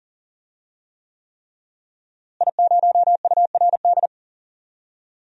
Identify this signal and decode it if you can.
Morse code, heard as a keyed tone, about 40 wpm, 700 Hz. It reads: I0URD
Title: DXCC-100 Call Signs - 5 Characters - Repeated 3x - 40wpm